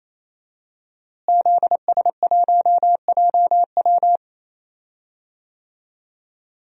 Morse code, heard as a keyed tone, about 28 wpm, 700 Hz. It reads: ZS1JW
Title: DXCC-100 Call Signs - 5 Characters - Repeated 3x - 28wpm